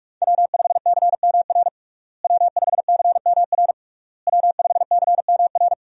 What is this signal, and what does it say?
Morse code, heard as a keyed tone, about 42 wpm, 700 Hz. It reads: W5CMR W5CMR W5CMR
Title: USA Call Signs - 5 Characters - Repeated 3x - 45wpm